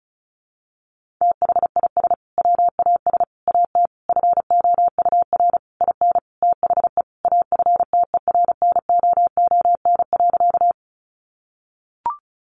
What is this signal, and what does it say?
Morse code, heard as a keyed tone, about 34 wpm, 700 Hz. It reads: THIS WAS AT FOUR IN THE AFTERNOON.
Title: Sentences from Top 1500 Words 35wpm